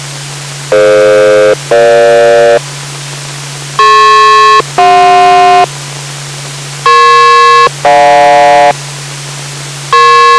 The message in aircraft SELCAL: DF-EG DP-CJ EP-GJ